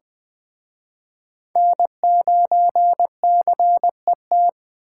Morse code, heard as a keyed tone, about 20 wpm, 700 Hz. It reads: N9CET
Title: Speed Racing - USA Call Signs 17wpm